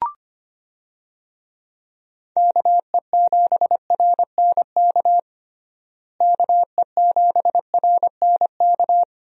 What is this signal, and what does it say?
Morse code, heard as a keyed tone, about 25 wpm, 700 Hz. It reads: KE7RNK KE7RNK
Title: USA Call Signs - 6 Characters - Repeated 2x - 25wpm